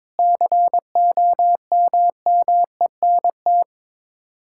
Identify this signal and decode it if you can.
Morse code, heard as a keyed tone, about 22 wpm, 700 Hz. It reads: COMMENT
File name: practice / 22wpm / Sets of 6 Words - 22wpm